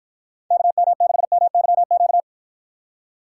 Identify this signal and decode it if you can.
Morse code, heard as a keyed tone, about 54 wpm, 700 Hz. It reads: KM6MYQ